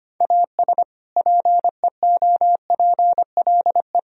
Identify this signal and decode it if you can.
Morse code, heard as a keyed tone, about 25 wpm, 700 Hz. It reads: AS PEOPLE
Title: Sentences from Top 1000 Words - Review Part 4 - 25wpm